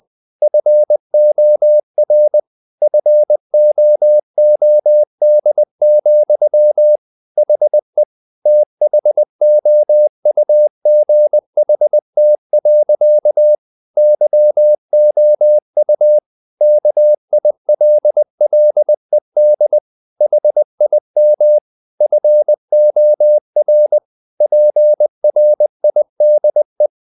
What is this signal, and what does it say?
Morse code, heard as a keyed tone, about 20 wpm, 600 Hz. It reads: FOR FOOD, HE THOUGHT. YOU KILLED HIM FOR PRIDE
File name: the old man and the sea part 2 20wpm